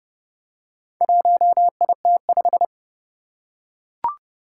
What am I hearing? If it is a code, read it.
Morse code, heard as a keyed tone, about 30 wpm, 700 Hz. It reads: 1IT5
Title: Four Challenging Letter-Number Combinations 30wpm